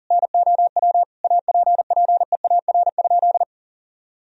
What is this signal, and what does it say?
Morse code, heard as a keyed tone, about 40 wpm, 700 Hz. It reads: NOW APPEAR?
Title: Sentences from Top 1000 Words - Review Part 2 - 40wpm